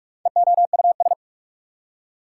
Morse code, heard as a keyed tone, about 45 wpm, 690 Hz. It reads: EOUS
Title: Prefixes and Suffixes 45wpm